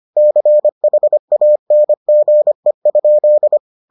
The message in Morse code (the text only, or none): CHANGE?